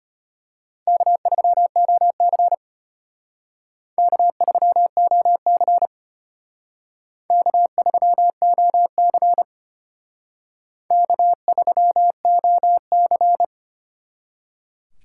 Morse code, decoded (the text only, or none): K3OC K3OC K3OC K3OC